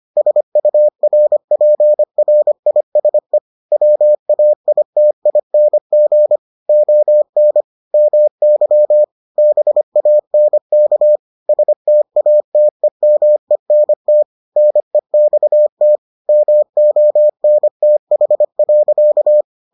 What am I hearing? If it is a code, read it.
Morse code, heard as a keyed tone, about 25 wpm, 600 Hz. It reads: SURPRISE WAITING ON MY BANK STATEMENT NEXT MONTH.